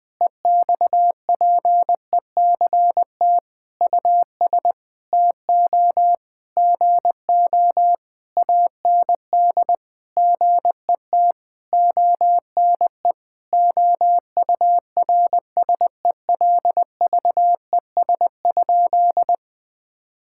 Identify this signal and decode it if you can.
Morse code, heard as a keyed tone, about 20 wpm, 700 Hz. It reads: EXPECT US TO GO AND GET ONE OURSELVES?
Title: Sentences from Top 1600 Words 20wpm